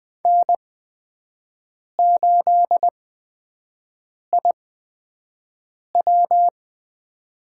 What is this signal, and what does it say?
Morse code, heard as a keyed tone, about 20 wpm, spaced as for 5 wpm, 700 Hz. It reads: N8IW
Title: USA Call Signs - 8x Character Spacing - 20wpm